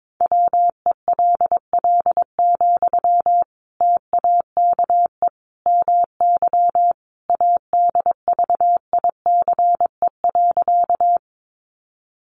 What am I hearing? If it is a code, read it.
Morse code, heard as a keyed tone, about 22 wpm, 700 Hz. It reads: WELL, TAKE MY ADVICE.
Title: Sentences from Top 1100 Words 22wpm